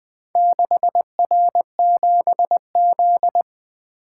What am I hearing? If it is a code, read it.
Morse code, heard as a keyed tone, about 20 wpm, 700 Hz. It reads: 6R7Z
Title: Four Letter-Number Combinations - 20wpm